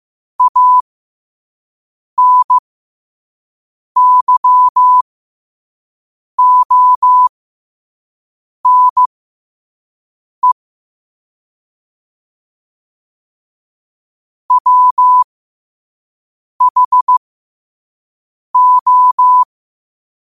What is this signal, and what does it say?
Morse code, heard as a keyed tone, about 15 wpm, spaced as for 5 wpm, 1000 Hz. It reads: ANYONE WHO
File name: qotdfiles / 5wpm